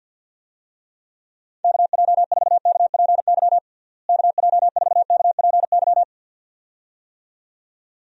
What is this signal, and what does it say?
Morse code, heard as a keyed tone, about 46 wpm, 700 Hz. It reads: KJ4KPY KJ4KPY
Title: USA Call Signs - 6 Characters - Repeated 2x - 50wpm